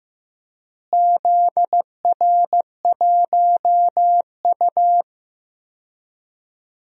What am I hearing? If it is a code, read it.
Morse code, heard as a keyed tone, about 15 wpm, 700 Hz. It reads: ZR1U
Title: DXCC-100 Call Signs - 4 Characters - Repeated 2x - 15wpm